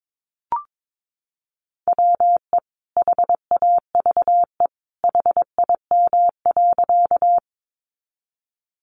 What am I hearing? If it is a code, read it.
Morse code, heard as a keyed tone, about 22 wpm, 700 Hz. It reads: WE HAVE HIM.
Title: Sentences from Top 100 Words 22wpm